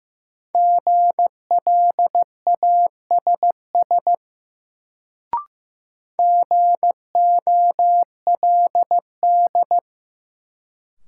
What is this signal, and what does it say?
Morse code, heard as a keyed tone, about 15 wpm, 700 Hz. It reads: GLASS GOLD
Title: Sentences from Top 1000 Words - Review Part 3 - New Word Drill - 15wpm